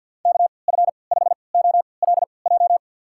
Morse code, heard as a keyed tone, about 53 wpm, spaced as for 29 wpm, 700 Hz. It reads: KF5OLJ
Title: USA Call Signs - 6 Characters - 3x Character Spacing - 50wpm